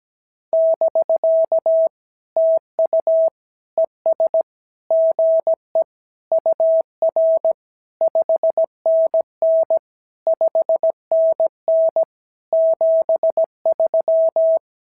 Morse code, heard as a keyed tone, about 17 wpm, 650 Hz. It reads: <BK> TU ES GE UR 5NN 5NN 73
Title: SOTA QSO Practice 17wpm